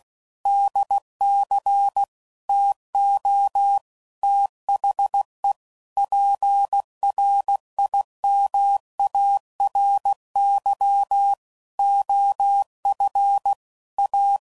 Morse code, heard as a keyed tone, about 16 wpm, 800 Hz. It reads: DC TO THE PRIMARY OF A
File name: CW Code Groups @15WPM / Technical Subjects / PwrSupply